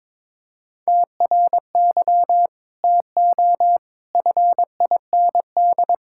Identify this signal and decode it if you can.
Morse code, heard as a keyed tone, about 22 wpm, 700 Hz. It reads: TRY TO FIND
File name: practice / 22wpm / Sentences with 3-Word Phrases - 22wpm